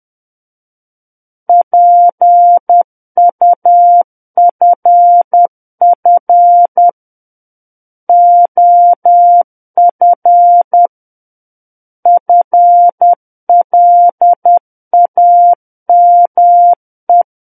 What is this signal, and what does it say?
Morse code, heard as a keyed tone, about 10 wpm, 700 Hz. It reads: PUFF OF FLAME
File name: Book One / 01-Chapter 1 - 10 WPM